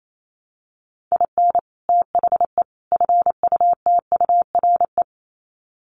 Morse code, heard as a keyed tone, about 28 wpm, 700 Hz. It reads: IN THE FUTURE
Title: Sets of 3 Words 28wpm